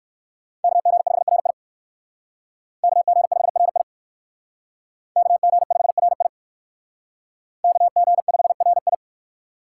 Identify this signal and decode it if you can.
Morse code, heard as a keyed tone, about 58 wpm, 700 Hz. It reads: KG5RI KG5RI KG5RI KG5RI